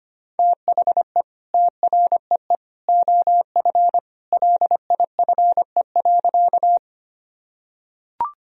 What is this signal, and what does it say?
Morse code, heard as a keyed tone, about 25 wpm, 700 Hz. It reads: THE TREE OF LIFE.
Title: Sentences from Top 1000 Words - Review Part 1 - 25wpm